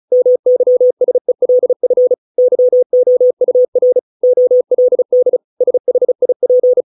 Morse code, heard as a keyed tone, about 34 wpm, 500 Hz. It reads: MYSELF YOUR OLD SHIP